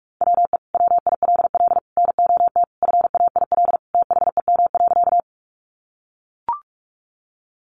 Morse code, heard as a keyed tone, about 44 wpm, 700 Hz. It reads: WE WILL NOT FAIL THEM.
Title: Sentences from Top 1100 Words 45wpm